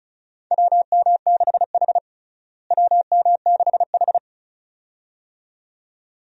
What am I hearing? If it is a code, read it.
Morse code, heard as a keyed tone, about 36 wpm, 700 Hz. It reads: WM6H WM6H
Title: USA Call Signs - 4 Characters - Repeated 2x - 35wpm